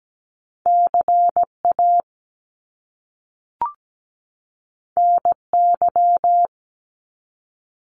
Morse code, heard as a keyed tone, about 17 wpm, 700 Hz. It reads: CA NY
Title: US States 17wpm